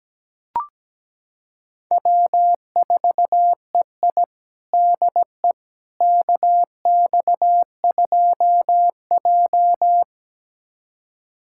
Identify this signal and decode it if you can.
Morse code, heard as a keyed tone, about 17 wpm, 700 Hz. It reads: W4EI DE KX2J
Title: USA Callsign DE Callsign - 4 Characters - 17wpm